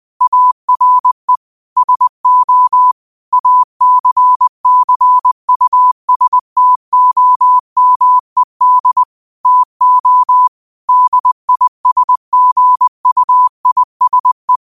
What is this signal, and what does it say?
Morse code, heard as a keyed tone, about 20 wpm, 1000 Hz. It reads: ARE SO ACCUSTOMED TO DISGUISE
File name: qotdfiles / 20wpm